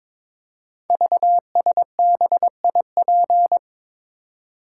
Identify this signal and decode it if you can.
Morse code, heard as a keyed tone, about 22 wpm, 690 Hz. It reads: VSBIP
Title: Code Groups - 5 Letter:Numbers - 22wpm